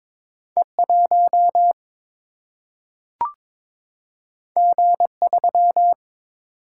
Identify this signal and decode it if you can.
Morse code, heard as a keyed tone, about 22 wpm, 700 Hz. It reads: E 1 G 3
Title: Single-Letter Single-Number 22wpm